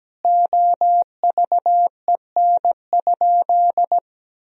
Morse code, heard as a keyed tone, about 17 wpm, 700 Hz. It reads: OVEN?
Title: Sentences from Top 1000 Words - Review Part 4 - 17wpm